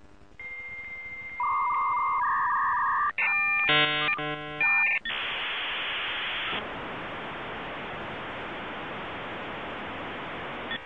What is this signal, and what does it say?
Dial-up modem handshake.